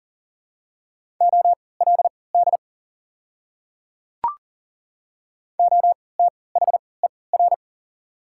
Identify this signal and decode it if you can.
Morse code, heard as a keyed tone, about 37 wpm, spaced as for 22 wpm, 700 Hz. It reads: OLD OTHER
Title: 3x Character Spacing - Top 200 Words - 40wpm